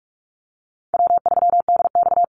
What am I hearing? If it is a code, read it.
Morse code, heard as a keyed tone, about 44 wpm, 700 Hz. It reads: W3DX
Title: NA SOTA Activator Callsigns 45wpm